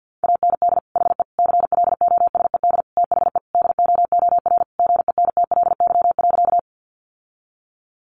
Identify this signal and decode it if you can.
Morse code, heard as a keyed tone, about 50 wpm, 700 Hz. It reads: AND HE CLOSED THE DOOR GENTLY.
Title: Sentences from Top 1500 Words 50wpm